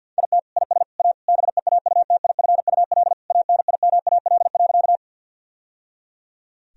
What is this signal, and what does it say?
Morse code, heard as a keyed tone, about 51 wpm, 700 Hz. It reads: IT IS A BEAUTIFUL ANIMAL.
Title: Sentences from Top 1000 Words - Review Part 1 - 50wpm